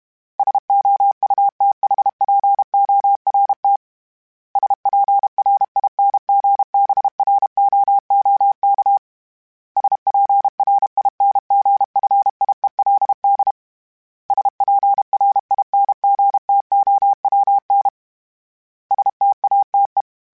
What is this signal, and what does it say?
Morse code, heard as a keyed tone, about 32 wpm, 800 Hz. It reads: SOUTHPORT SPRINGBROOK SPRINGFIELD SPRINGTOWN STATE
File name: fromSMs / FromSMs mp3s / Cities@30